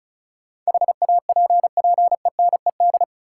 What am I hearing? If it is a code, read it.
Morse code, heard as a keyed tone, about 36 wpm, 700 Hz. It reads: HAPPENED
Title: Sentences from Top 1000 Words - Review Part 2 - New Word Drill - 35wpm